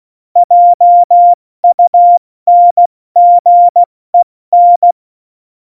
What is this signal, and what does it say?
Morse code, heard as a keyed tone, about 16 wpm, 700 Hz. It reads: JUNGEN
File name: Morsekurs / Witze / Witz 19. Februar 2019, 70 F80